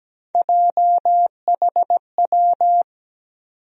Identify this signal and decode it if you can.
Morse code, heard as a keyed tone, about 17 wpm, 700 Hz. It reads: JHW
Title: USA Call Sign Postfixes - 3 Letters 17wpm